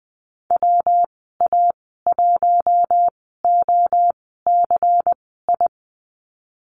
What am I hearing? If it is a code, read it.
Morse code, heard as a keyed tone, about 20 wpm, spaced as for 14 wpm, 700 Hz. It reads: WA1OCI
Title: 179 Lesson 29g - Slash - Call Signs 20wpm